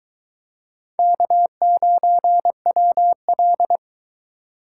Morse code, heard as a keyed tone, about 23 wpm, 700 Hz. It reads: K9WL